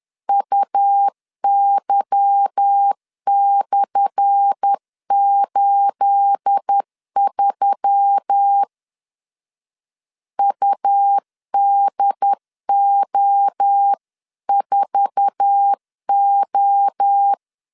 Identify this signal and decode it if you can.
Morse code, heard as a keyed tone, about 10 wpm, 800 Hz. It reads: UY/83 UDO4O
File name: LEZIONI / 10 SIGNS / 10A SLASH 10wpm 2808